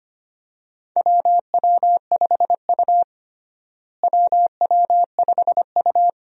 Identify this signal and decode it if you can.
Morse code, heard as a keyed tone, about 25 wpm, 700 Hz. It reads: WW5U WW5U
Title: USA Call Signs - 4 Characters - Repeated 2x - 25wpm